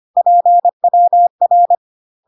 Morse code, heard as a keyed tone, about 25 wpm, 700 Hz. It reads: PWR
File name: cw / CW common abbreviations alphabetical repeated 3 times 25 25wpm ik1hge